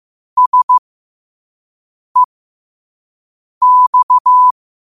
Morse code, heard as a keyed tone, about 15 wpm, spaced as for 5 wpm, 1000 Hz. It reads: SEX
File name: qotdfiles / 5wpm